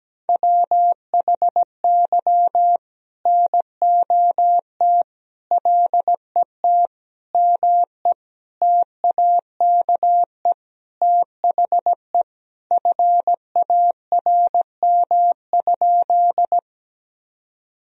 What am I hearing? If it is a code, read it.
Morse code, heard as a keyed tone, about 17 wpm, 700 Hz. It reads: WHY NOT LET ME TAKE THE FARM?
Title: Sentences from Top 1000 Words - Review Part 2 - 17wpm